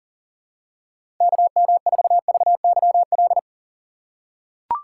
Morse code, heard as a keyed tone, about 40 wpm, 700 Hz. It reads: KM4VYL
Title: USA Call Signs - 6 Characters - 3x Character Spacing - 40wpm